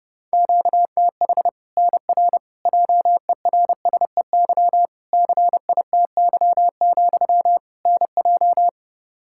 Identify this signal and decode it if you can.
Morse code, heard as a keyed tone, about 30 wpm, 700 Hz. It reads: QTH NR JERSEY CITY, NJ
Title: K1USN SST QSO Practice 30wpm